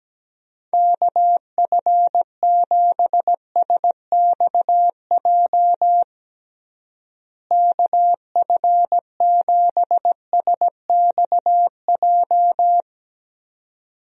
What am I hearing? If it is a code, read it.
Morse code, heard as a keyed tone, about 17 wpm, 700 Hz. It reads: KF7SXJ KF7SXJ
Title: USA Call Signs - 6 Characters - Repeated 2x - 17wpm